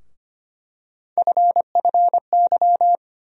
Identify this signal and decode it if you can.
Morse code, heard as a keyed tone, about 25 wpm, 700 Hz. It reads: FFY
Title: USA Call Sign Postfixes - 3 Letters 25wpm